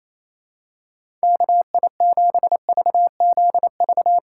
Morse code, heard as a keyed tone, about 28 wpm, 700 Hz. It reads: KI7VZV
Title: USA Call Signs - 6 Characters - 4x Character Spacing - 28wpm